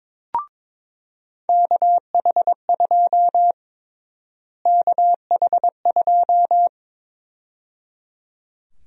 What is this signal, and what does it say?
Morse code, heard as a keyed tone, about 22 wpm, 700 Hz. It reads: KH2 KH2
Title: USA Call Sign Prefixes - Repeated 2x - 22wpm